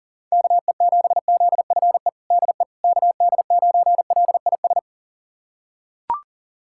Morse code, heard as a keyed tone, about 40 wpm, 700 Hz. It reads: KE7ZFE DE KD9LIS
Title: USA Callsign DE Callsign 40wpm